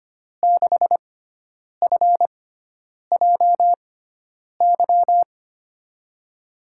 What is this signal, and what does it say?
Morse code, heard as a keyed tone, about 25 wpm, spaced as for 9 wpm, 700 Hz. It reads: 6FJY
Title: USA Call Signs - 6 Characters - 6x Character Spacing - 25wpm